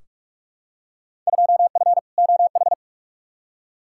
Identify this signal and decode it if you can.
Morse code, heard as a keyed tone, about 46 wpm, 700 Hz. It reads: 2F OH
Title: Field Day Exchanges - 45wpm